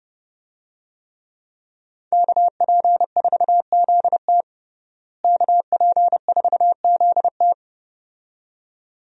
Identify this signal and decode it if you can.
Morse code, heard as a keyed tone, about 30 wpm, 700 Hz. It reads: KP4ZT KP4ZT
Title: USA Call Signs - 5 Characters - Repeated 2x - 30wpm